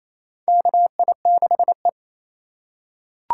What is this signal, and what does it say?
Morse code, heard as a keyed tone, about 28 wpm, 700 Hz. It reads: KI6E
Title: USA Call Signs - 4 Characters - 2x Character Spacing - 28wpm